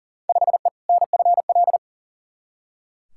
Morse code, heard as a keyed tone, about 40 wpm, 700 Hz. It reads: 5E NFL
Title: Field Day Exchanges Encore - 40wpm